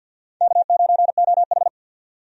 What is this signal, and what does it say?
Morse code, heard as a keyed tone, about 52 wpm, 700 Hz. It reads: K9OH
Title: USA Call Signs - 4 Characters - 50wpm